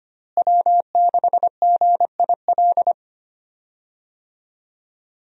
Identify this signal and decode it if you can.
Morse code, heard as a keyed tone, about 25 wpm, 700 Hz. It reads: W6GIL
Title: USA Call Signs - 5 Characters - Repeated 2x - 25wpm